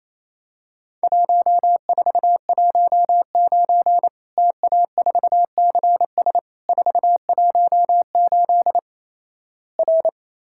Morse code, heard as a keyed tone, about 28 wpm, 700 Hz. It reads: 1419 TA4CS 418 R
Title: Worked All Europe DX Contest - QTC Exchange - 28wpm